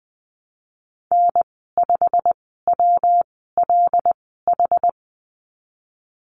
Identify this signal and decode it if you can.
Morse code, heard as a keyed tone, about 20 wpm, spaced as for 15 wpm, 700 Hz. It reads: N5WLH
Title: 179 Lesson 29g - Slash - Call Signs 20wpm